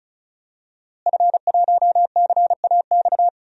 Morse code, heard as a keyed tone, about 36 wpm, 700 Hz. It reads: F1CAX
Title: DXCC-100 Call Signs - 5 Characters - Repeated 2x - 35wpm